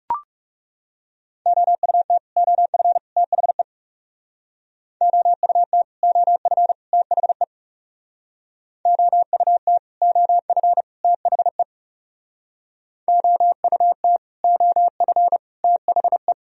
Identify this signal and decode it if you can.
Morse code, heard as a keyed tone, about 37 wpm, 700 Hz. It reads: OUT OF THE OUT OF THE OUT OF THE OUT OF THE
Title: Speed-Racing - Sets of 3 Words 30wpm